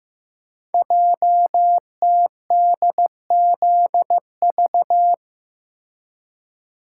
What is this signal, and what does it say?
Morse code, heard as a keyed tone, about 15 wpm, 700 Hz. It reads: JTDZV
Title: Code Groups - 5 Letters - 15wpm